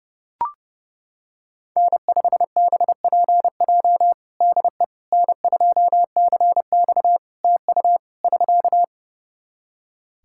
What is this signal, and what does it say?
Morse code, heard as a keyed tone, about 30 wpm, 700 Hz. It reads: N5BPJ DE N2CX TU <SK>
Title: K1USN SST QSO Practice 30wpm